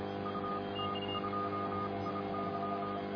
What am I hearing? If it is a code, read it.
Morse code, heard as a keyed tone, about 27 wpm, spaced as for 24 wpm, 1280 Hz, with noise. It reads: UT2IY